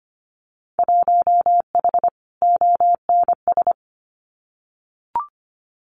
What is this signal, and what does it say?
Morse code, heard as a keyed tone, about 25 wpm, 700 Hz. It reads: 1H ONS
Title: Winter Field Day Exchanges 25wpm